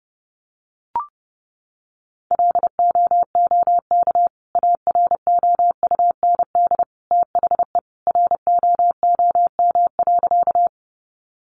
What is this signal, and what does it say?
Morse code, heard as a keyed tone, about 30 wpm, 700 Hz. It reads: LOOK AROUND THE ROOM.
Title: Sentences from Top 600 Words 30wpm